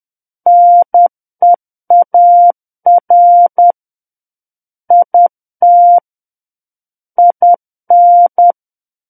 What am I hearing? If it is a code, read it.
Morse code, heard as a keyed tone, about 10 wpm, 700 Hz. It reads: NEAR IT IN
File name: Book One / 01-Chapter 1 - 10 WPM